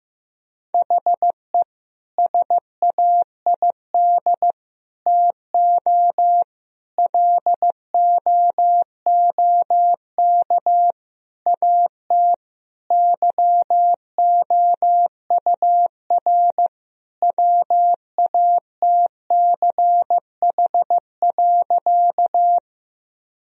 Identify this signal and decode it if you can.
Morse code, heard as a keyed tone, about 15 wpm, 700 Hz. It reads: HE SAID TO LOOK AT YOUR WATCH.
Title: Sentences from Top 1000 Words - Review Part 1 - 15wpm